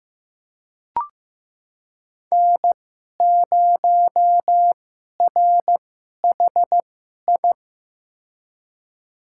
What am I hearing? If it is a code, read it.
Morse code, heard as a keyed tone, about 15 wpm, spaced as for 11 wpm, 700 Hz. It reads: N0RHI
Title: USA Call Signs - 5 Characters - 2x Character Spacing - 15wpm